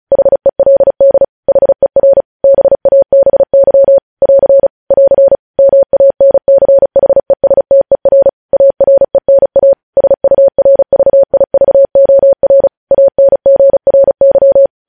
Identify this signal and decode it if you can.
Morse code, heard as a keyed tone, about 36 wpm, 550 Hz. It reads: HELD HER BABY <AR> <AR> MANCHESTER ARENA SURVIVOR ANGRY
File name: podcast / News headlines-1x-35x35